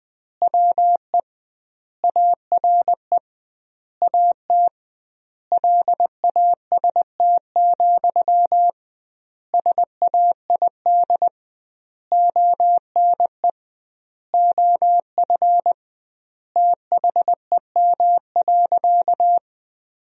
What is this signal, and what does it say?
Morse code, heard as a keyed tone, about 20 wpm, 700 Hz. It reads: WE ARE AT LAST, SAID ONE OF THEM.
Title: 2x Word-Spacing - Sentences from Top 300 Words - 20wpm